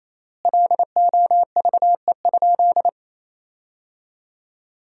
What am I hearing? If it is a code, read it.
Morse code, heard as a keyed tone, about 28 wpm, 700 Hz. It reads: LOVE?
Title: 3x Word-Spacing - Sentences from Top 500 Words - 28wpm